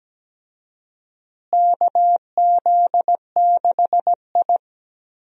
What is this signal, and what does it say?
Morse code, heard as a keyed tone, about 17 wpm, 700 Hz. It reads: KZ6I
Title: USA Call Signs - 4 Characters - Repeated 3x - 17wpm